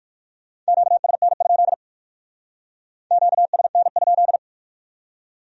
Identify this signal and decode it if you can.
Morse code, heard as a keyed tone, about 56 wpm, 700 Hz. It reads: QSN? QSN?
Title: Speed-Racing Q-Codes - 45wpm